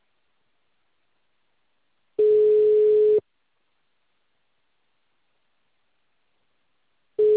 Ringback tone (Europe, 425 Hz).
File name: out- -2044- - - .10025